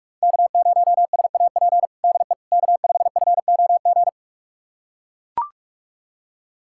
Morse code, heard as a keyed tone, about 41 wpm, 700 Hz. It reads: K0SAP DE K5FOG
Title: USA Callsign DE Callsign - 5 Characters - 45wpm